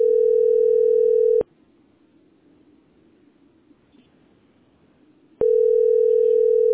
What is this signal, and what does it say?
Ringback tone (North America), 440 + 480 Hz.